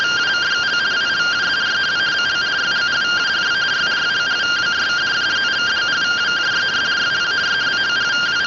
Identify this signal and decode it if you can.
Two-tone teleprinter signal, about 170 Hz shift.